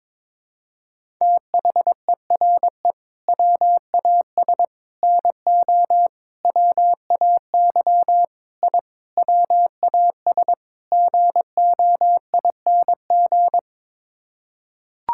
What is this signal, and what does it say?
Morse code, heard as a keyed tone, about 22 wpm, 700 Hz. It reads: THERE WAS NO WAY I WAS GOING
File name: practice / 22wpm / Sets of 7 Words - 22wpm